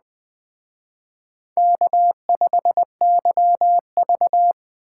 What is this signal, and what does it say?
Morse code, heard as a keyed tone, about 20 wpm, 700 Hz. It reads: K5YV
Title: Speed Racing - USA 1x2 Call Signs 17wpm